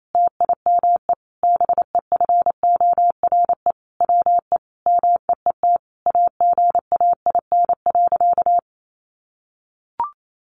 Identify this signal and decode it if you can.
Morse code, heard as a keyed tone, about 28 wpm, 700 Hz. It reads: TIME BEFORE WE MEET AGAIN.
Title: Sentences from Top 1100 Words 28wpm